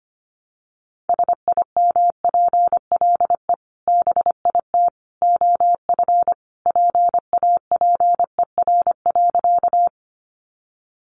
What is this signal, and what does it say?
Morse code, heard as a keyed tone, about 25 wpm, 700 Hz. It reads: SIMPLE BIT OF PAPER.
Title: Sentences from Top 900 Words 25wpm